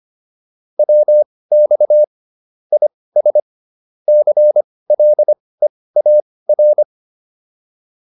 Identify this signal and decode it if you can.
Morse code, heard as a keyed tone, about 25 wpm, spaced as for 18 wpm, 600 Hz. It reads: WX IS CLEAR
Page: intermediate-practice-files